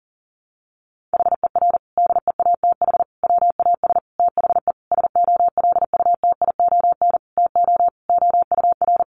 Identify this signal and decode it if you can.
Morse code, heard as a keyed tone, about 40 wpm, 700 Hz. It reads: HER DEATH WAS THE SOLUTION TO OUR
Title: Sentences from Top 1300 Words 40wpm